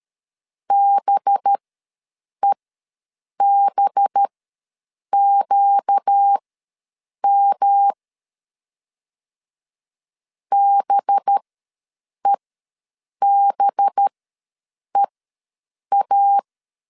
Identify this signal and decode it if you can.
Morse code, heard as a keyed tone, about 13 wpm, spaced as for 7 wpm, 800 Hz. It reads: BEBQM BEBEA
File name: 5 LB ws 12wpm